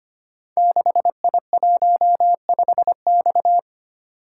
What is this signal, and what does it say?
Morse code, heard as a keyed tone, about 25 wpm, 700 Hz. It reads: 6I15X
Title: Code Groups - 5 Letter:Numbers - 25wpm